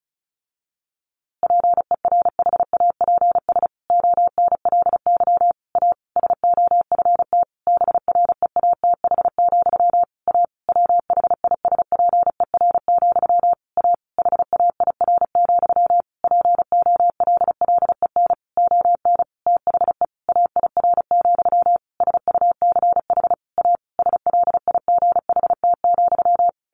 Morse code, heard as a keyed tone, about 34 wpm, 700 Hz. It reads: PERHAPS ONLY A SOFT BREATH, A WHISPER, A HAIR, POLLEN ON THE AIR, SUCH A SLIGHT,
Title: A Sound Of Thunder 35wpm